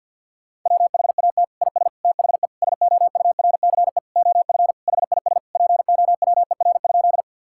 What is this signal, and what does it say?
Morse code, heard as a keyed tone, about 49 wpm, 700 Hz. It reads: WHAT IS THE SOURCE OF HIS POWER?
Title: Sentences from Top 1000 Words - Review Part 5 - 50wpm